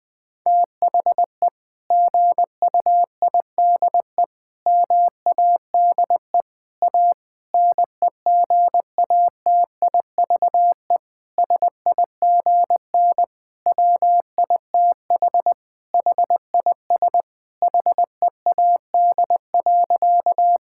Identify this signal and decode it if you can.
Morse code, heard as a keyed tone, about 20 wpm, 700 Hz. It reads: THE GUIDE MADE A NEGATIVE SIGN WITH HIS HEAD.
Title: Sentences from Top 1000 Words - Review Part 6 - 20wpm